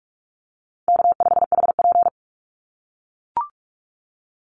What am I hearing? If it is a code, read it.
Morse code, heard as a keyed tone, about 44 wpm, 700 Hz. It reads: K5HP
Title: Easy Call Signs - 45wpm